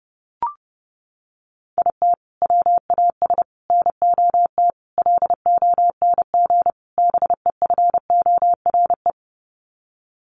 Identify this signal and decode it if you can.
Morse code, heard as a keyed tone, about 28 wpm, 700 Hz. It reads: IT WAS NOT LONG BEFORE
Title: Sets of 5 Words Encore2 30wpm